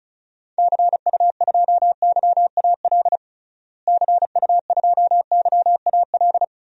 Morse code, heard as a keyed tone, about 36 wpm, 700 Hz. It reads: CU2YAL CU2YAL
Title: DXCC-100 Call Signs - 6 Characters - Repeated 2x - 35wpm